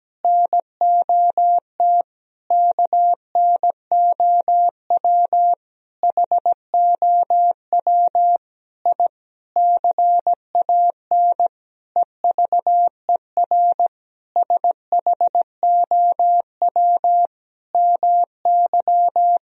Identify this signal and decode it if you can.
Morse code, heard as a keyed tone, about 17 wpm, 700 Hz. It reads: NOT KNOW HOW I CAN EVER SHOW MY
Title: Sentences from Top 1000 Words - Review Part 2 - 17wpm